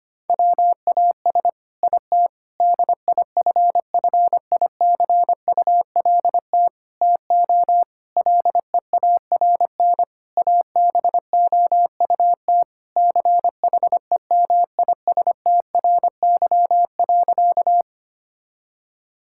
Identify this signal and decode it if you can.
Morse code, heard as a keyed tone, about 25 wpm, 700 Hz. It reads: WAS IT DIFFICULT TO LEARN ABOUT CHEMISTRY.
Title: Sentences from Top 1000 Words - Review Part 6 - 25wpm